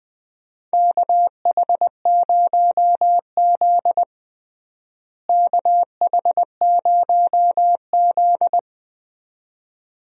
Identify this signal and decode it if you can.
Morse code, heard as a keyed tone, about 21 wpm, 700 Hz. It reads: KH0Z KH0Z
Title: USA Call Signs - 4 Characters - Repeated 2x - 20wpm